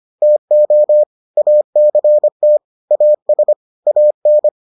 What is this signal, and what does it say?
Morse code, heard as a keyed tone, about 25 wpm, 600 Hz. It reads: TO ACT AS AN